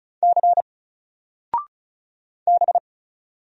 Morse code, heard as a keyed tone, about 36 wpm, 700 Hz. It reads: C B
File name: Single-Letter-Number-35wpm